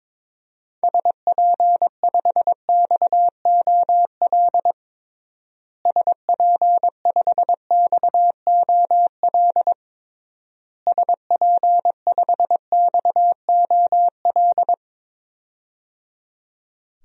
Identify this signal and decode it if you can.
Morse code, heard as a keyed tone, about 21 wpm, 700 Hz. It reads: SP5XOL SP5XOL SP5XOL
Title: DXCC-100 Call Signs - 6 Characters - Repeated 3x - 22wpm